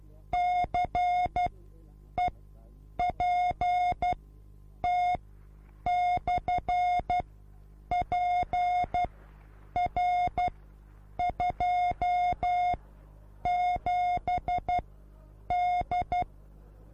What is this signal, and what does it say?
Morse code, heard as a keyed tone, about 12 wpm, spaced as for 8 wpm, 720 Hz. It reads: CEPT/PR27D